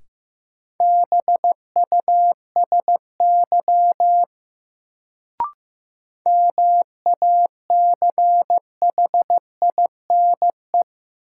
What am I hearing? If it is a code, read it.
Morse code, heard as a keyed tone, about 15 wpm, 700 Hz. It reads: BUSY MACHINE
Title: Sentences from Top 1000 Words - Review Part 2 - New Word Drill - 15wpm